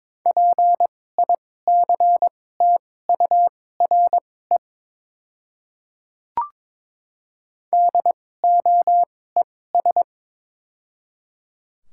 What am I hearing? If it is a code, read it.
Morse code, heard as a keyed tone, about 21 wpm, spaced as for 16 wpm, 700 Hz. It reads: PICTURE DOES
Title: 2x Character Spacing - Top 200 Words - 22wpm